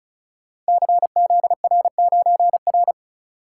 Morse code, heard as a keyed tone, about 36 wpm, 700 Hz. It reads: CZR9R
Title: Code Groups - 5 Letter:Numbers - 35wpm